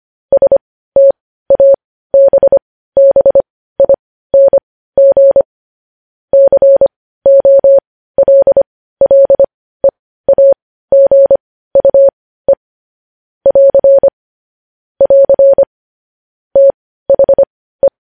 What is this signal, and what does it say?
Morse code, heard as a keyed tone, about 25 wpm, spaced as for 15 wpm, 550 Hz. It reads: STABBING COLLEAGUE <AR> <AR> THE
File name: podcast / News headlines-1x-25x15